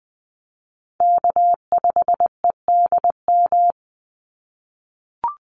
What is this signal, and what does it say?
Morse code, heard as a keyed tone, about 20 wpm, 700 Hz. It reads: K5EDM
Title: Call Signs - 20wpm